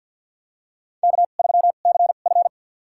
Morse code, heard as a keyed tone, about 47 wpm, spaced as for 35 wpm, 700 Hz. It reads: K3CF
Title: USA Call Signs - 4 Characters - 2x Character Spacing - 50wpm